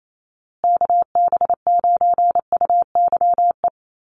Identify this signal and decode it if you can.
Morse code, heard as a keyed tone, about 28 wpm, 700 Hz. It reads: KB9UYE
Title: USA Call Signs Repeated 3x 28wpm